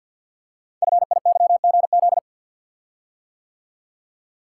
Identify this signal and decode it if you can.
Morse code, heard as a keyed tone, about 50 wpm, 690 Hz. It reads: FIYGZ
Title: Code Groups - 5 Letters - 50wpm